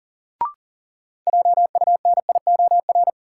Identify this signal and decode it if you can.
Morse code, heard as a keyed tone, about 40 wpm, 700 Hz. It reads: JUNIOR
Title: Sentences from Top 1500 Words - New Word Drill Alt - 40wpm